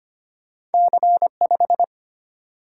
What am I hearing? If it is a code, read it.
Morse code, heard as a keyed tone, about 32 wpm, 700 Hz. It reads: C5